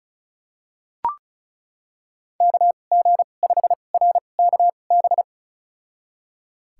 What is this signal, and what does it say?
Morse code, heard as a keyed tone, about 36 wpm, spaced as for 25 wpm, 700 Hz. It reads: KG5RKB
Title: USA Call Signs - 6 Characters - 2x Character Spacing - 35wpm